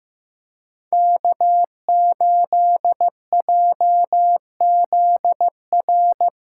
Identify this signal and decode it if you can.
Morse code, heard as a keyed tone, about 16 wpm, 700 Hz. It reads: K8JZR
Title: USA Callsign DE Callsign - 5 Characters - 15wpm